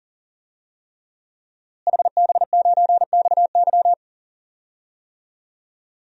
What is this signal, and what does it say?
Morse code, heard as a keyed tone, about 40 wpm, 700 Hz. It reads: HB9XY
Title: DXCC-100 Call Signs - 5 Characters - Repeated 3x - 40wpm